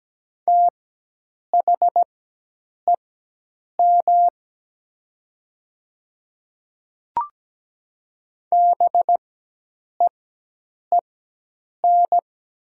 Morse code, heard as a keyed tone, about 17 wpm, spaced as for 8 wpm, 700 Hz. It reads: THEM BEEN
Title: 4x Character Spacing - Top 200 Words - 17wpm